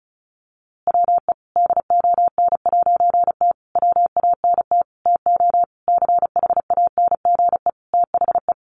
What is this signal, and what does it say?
Morse code, heard as a keyed tone, about 34 wpm, 700 Hz. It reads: WE DON'T WANT TO CHANGE THE
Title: A Sound Of Thunder 35wpm